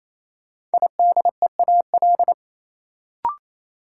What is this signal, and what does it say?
Morse code, heard as a keyed tone, about 28 wpm, 700 Hz. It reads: IDEAL
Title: Sentences from Top 1400 Words - New Word Drill - 28wpm